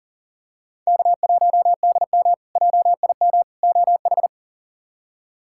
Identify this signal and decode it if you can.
Morse code, heard as a keyed tone, about 37 wpm, 700 Hz. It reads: K1DM JIM OH
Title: K1USN SST QSO Practice - 40wpm